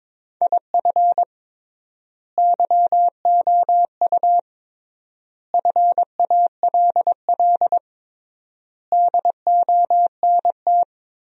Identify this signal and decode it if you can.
Morse code, heard as a keyed tone, about 22 wpm, 700 Hz. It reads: IF YOU FALL DONT
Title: 3x Word-Spacing - Sentences from Top 500 Words - 22wpm